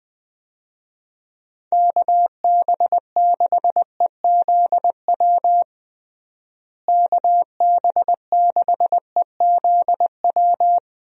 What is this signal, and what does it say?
Morse code, heard as a keyed tone, about 20 wpm, 700 Hz. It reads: KB6EZW KB6EZW
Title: USA Call Signs - 6 Characters - Repeated 2x - 20wpm